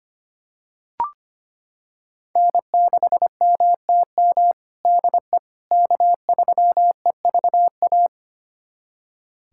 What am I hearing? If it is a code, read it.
Morse code, heard as a keyed tone, about 25 wpm, 700 Hz. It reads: N6MTM DE K3EVA
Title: USA Callsign DE Callsign - 5 Characters - 25wpm